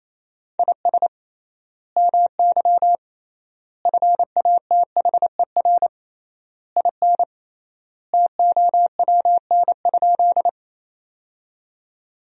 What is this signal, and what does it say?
Morse code, heard as a keyed tone, about 28 wpm, 700 Hz. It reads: IS MY FATHER IN TOWN?
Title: 3x Word-Spacing - Sentences from Top 500 Words - 28wpm